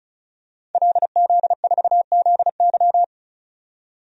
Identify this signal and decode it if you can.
Morse code, heard as a keyed tone, about 34 wpm, 700 Hz. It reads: LZ4ZY
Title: DXCC-100 Call Signs - 5 Characters - 35wpm